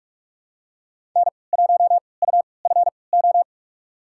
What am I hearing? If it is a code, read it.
Morse code, heard as a keyed tone, about 47 wpm, spaced as for 26 wpm, 700 Hz. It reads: N1UFO
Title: USA Call Signs - 3x Character Spacing - 45wpm